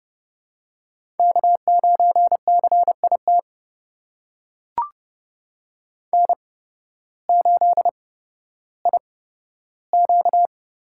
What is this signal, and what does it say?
Morse code, heard as a keyed tone, about 32 wpm, spaced as for 14 wpm, 700 Hz. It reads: K9CIT N8IQ
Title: USA Call Signs - 5 Characters - 8x Character Spacing - 30wpm